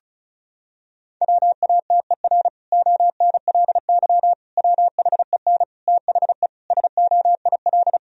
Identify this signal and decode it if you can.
Morse code, heard as a keyed tone, about 36 wpm, 700 Hz. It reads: WATER ONLY WHEN THE SOIL
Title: Sentences from Top 1000 Words - Review Part 4 - 35wpm